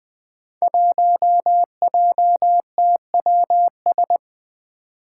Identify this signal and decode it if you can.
Morse code, heard as a keyed tone, about 20 wpm, 700 Hz. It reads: 1JTWS
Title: Five Challenging Letter-Number Combinations 20wpm